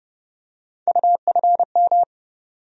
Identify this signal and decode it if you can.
Morse code, heard as a keyed tone, about 30 wpm, 700 Hz. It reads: UFM
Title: USA Call Sign Postfixes - 3 Letters 30wpm